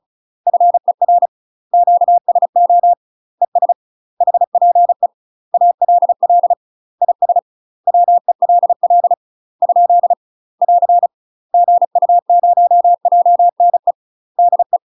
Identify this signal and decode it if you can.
Morse code, heard as a keyed tone, about 35 wpm, 700 Hz. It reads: FER QSO ES HPE ALL IS WELL ? <AR> GU0JNE DE
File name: UK MASTER QSOs TRAINING / 35WPM / UK MASTER QSOs 0013